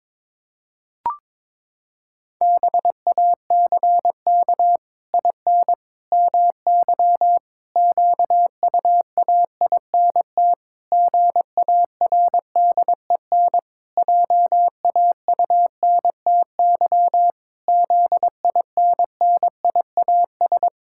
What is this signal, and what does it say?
Morse code, heard as a keyed tone, about 23 wpm, 700 Hz. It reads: BACK IN MY QUAINT GARDEN JAUNTY ZINNIAS
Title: Pangram Challenge v2 - 22wpm